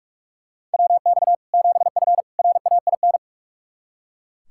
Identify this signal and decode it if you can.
Morse code, heard as a keyed tone, about 43 wpm, 700 Hz. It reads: WX 7F RAIN
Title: Practice QSO - Exchange 2 - Info - 45wpm